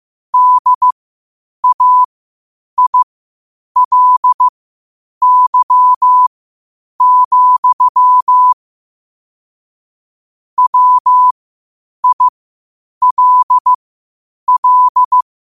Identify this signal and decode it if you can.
Morse code, heard as a keyed tone, about 15 wpm, spaced as for 8 wpm, 1000 Hz. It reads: DAILY, WILL
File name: qotdfiles / 8wpm